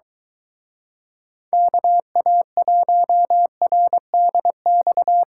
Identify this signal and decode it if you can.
Morse code, heard as a keyed tone, about 23 wpm, 700 Hz. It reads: KA1RDX